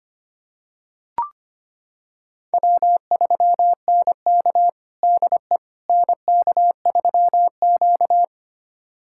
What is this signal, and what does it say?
Morse code, heard as a keyed tone, about 25 wpm, 700 Hz. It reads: W3NK DE NK3Q
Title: USA Callsign DE Callsign - 4 Characters - 25wpm